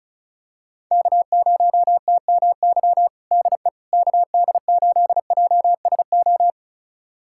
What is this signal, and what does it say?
Morse code, heard as a keyed tone, about 34 wpm, 690 Hz. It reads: K0TMY DE KD8JSO
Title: USA Callsign DE Callsign 35wpm